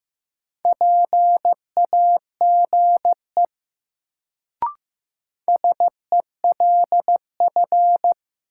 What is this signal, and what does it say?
Morse code, heard as a keyed tone, about 15 wpm, 700 Hz. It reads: PAGE SELF
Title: Sentences from Top 1000 Words - Review Part 1 - New Word Drill - 15wpm